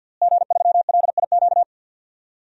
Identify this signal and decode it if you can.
Morse code, heard as a keyed tone, about 52 wpm, 700 Hz. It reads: G3LIQ
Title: DXCC-100 Call Signs - 5 Characters - Repeated 3x - 50wpm